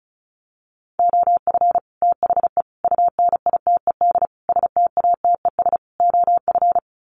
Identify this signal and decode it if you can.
Morse code, heard as a keyed tone, about 34 wpm, 700 Hz. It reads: OF THE UNITED STATES OF
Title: Sets of 5 Words Encore 35wpm